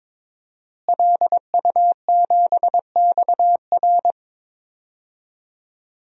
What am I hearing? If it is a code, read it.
Morse code, heard as a keyed tone, about 22 wpm, 700 Hz. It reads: LU7XR
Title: DXCC-100 Call Signs - 5 Characters - Repeated 3x - 22wpm